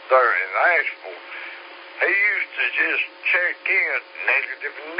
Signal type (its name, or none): Emergency Alert System (SAME) tones